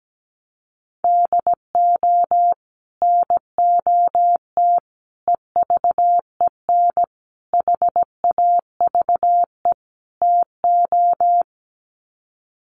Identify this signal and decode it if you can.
Morse code, heard as a keyed tone, about 17 wpm, 700 Hz. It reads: DO NOT EVEN HAVE TO
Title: Sets of 5 Words Encore2 17wpm